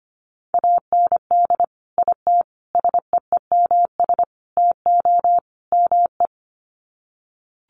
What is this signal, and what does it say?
Morse code, heard as a keyed tone, about 25 wpm, 700 Hz. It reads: AND IT SEEMS TO ME
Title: Sets of 5 Words Encore2 25wpm